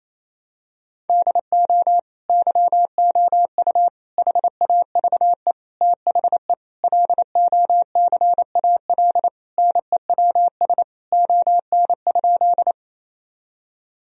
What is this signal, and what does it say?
Morse code, heard as a keyed tone, about 28 wpm, 700 Hz. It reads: DO YOU HAVE THE LOCAL NEWS ON?
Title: Sentences from Top 1000 Words - Review Part 3 - 28wpm